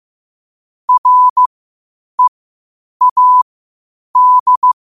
Morse code, heard as a keyed tone, about 15 wpm, spaced as for 8 wpm, 1000 Hz. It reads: READ
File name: qotdfiles / 8wpm